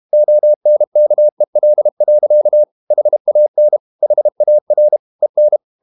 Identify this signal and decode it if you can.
Morse code, heard as a keyed tone, about 32 wpm, 600 Hz. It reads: ONKEL. HAN HAR EN
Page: Ferie 32 wpm